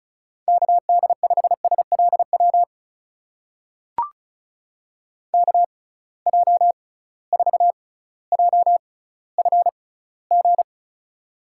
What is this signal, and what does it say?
Morse code, heard as a keyed tone, about 36 wpm, spaced as for 18 wpm, 700 Hz. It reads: KD5SLW KJ4JFG
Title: USA Call Signs - 6 Characters - 6x Character Spacing - 35wpm